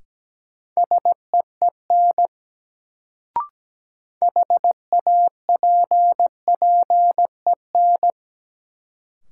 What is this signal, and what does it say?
Morse code, heard as a keyed tone, about 17 wpm, 700 Hz. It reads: SEEN HAPPEN
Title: Sets of 5 Words Encore - New Word Drill - 17wpm